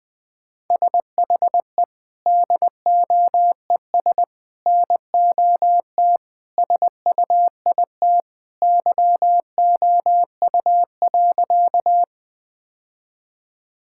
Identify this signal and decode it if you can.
Morse code, heard as a keyed tone, about 20 wpm, 700 Hz. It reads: SHE DOES NOT SUIT YOU.
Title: Sentences from Top 1600 Words 20wpm